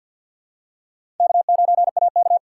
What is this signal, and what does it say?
Morse code, heard as a keyed tone, about 53 wpm, 700 Hz. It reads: K9AK
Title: USA Call Signs - 4x Character Spacing - 50wpm